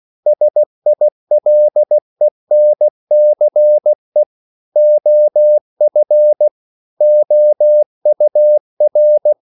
Morse code, heard as a keyed tone, about 16 wpm, 600 Hz. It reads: SILENCE OF OUR
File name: CW / Practice / EnglishQuotations0116